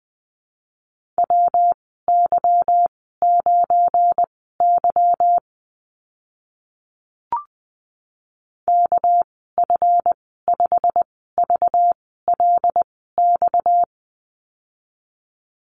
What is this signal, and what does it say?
Morse code, heard as a keyed tone, about 20 wpm, spaced as for 15 wpm, 700 Hz. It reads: WY9Y KF5VLX
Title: 238 Lesson 38g - Letter q - Call Signs 20wpm